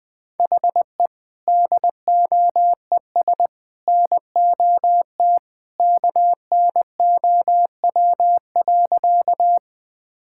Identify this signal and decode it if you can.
Morse code, heard as a keyed tone, about 20 wpm, 700 Hz. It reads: HE DOES NOT KNOW.
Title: Sentences with 3-Word Phrases - Spoken Beginning - 20wpm